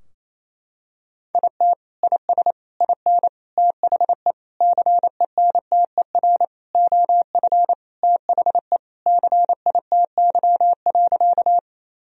Morse code, heard as a keyed tone, about 28 wpm, 690 Hz. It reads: IT IS IN THE CENTER OF THE CITY.
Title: Sentences from Top 1000 Words - Review Part 2 - 28wpm